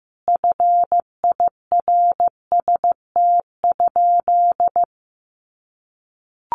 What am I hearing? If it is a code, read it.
Morse code, heard as a keyed tone, about 15 wpm, 700 Hz. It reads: FIRST?
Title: Sentences from Top 100 Words 15wpm